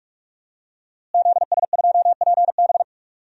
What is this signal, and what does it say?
Morse code, heard as a keyed tone, about 46 wpm, 700 Hz. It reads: ZS2PB
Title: DXCC-100 Call Signs - 5 Characters - Repeated 2x - 45wpm